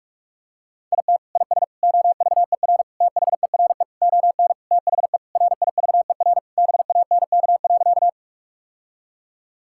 Morse code, heard as a keyed tone, about 41 wpm, 700 Hz. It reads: IT IS OVER THERE ON THE RIVER BANK.
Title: Sentences from Top 1000 Words - Review Part 6 - 45wpm